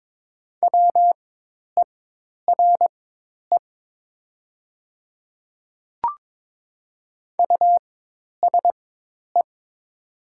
Morse code, heard as a keyed tone, about 22 wpm, spaced as for 10 wpm, 700 Hz. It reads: WERE USE
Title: 4x Character Spacing - Top 100 Words - 22wpm